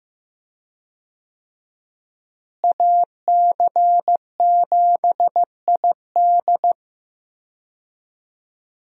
Morse code, heard as a keyed tone, about 15 wpm, 700 Hz. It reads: AC7ID
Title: USA Call Signs - 5 Characters - Repeated 2x - 15wpm